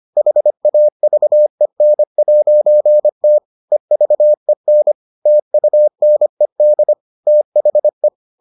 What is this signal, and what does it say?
Morse code, heard as a keyed tone, about 25 wpm, 600 Hz. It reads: HAVEN'T EVEN TUNED THE